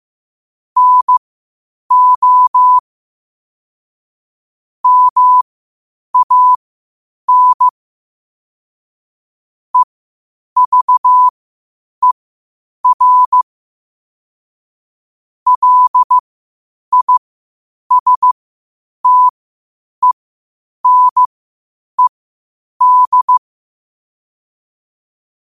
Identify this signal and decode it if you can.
Morse code, heard as a keyed tone, about 15 wpm, spaced as for 8 wpm, 1000 Hz. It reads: NO MAN EVER LISTENED
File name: qotdfiles / 8wpm